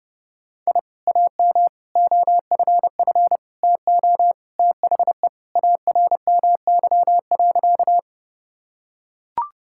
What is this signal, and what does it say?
Morse code, heard as a keyed tone, about 30 wpm, 700 Hz. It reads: I AM OFF TO THE ARMY.
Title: Sentences from Top 1000 Words - Review Part 5 - 30wpm